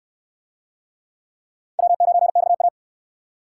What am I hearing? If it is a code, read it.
Morse code, heard as a keyed tone, about 62 wpm, 700 Hz. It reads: W9XA